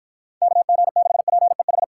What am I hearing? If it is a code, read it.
Morse code, heard as a keyed tone, about 53 wpm, 700 Hz. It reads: KG6PEH